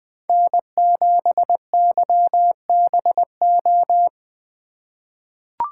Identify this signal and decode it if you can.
Morse code, heard as a keyed tone, about 20 wpm, 700 Hz. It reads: N7YBO
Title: USA Call Signs - 5 Characters - 2x Character Spacing - 20wpm